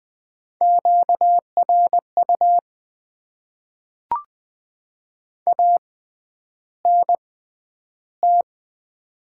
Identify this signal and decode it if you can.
Morse code, heard as a keyed tone, about 20 wpm, spaced as for 10 wpm, 700 Hz. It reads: QRU ANT
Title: Top 90 QSO Elements - 6x Character Spacing - 20wpm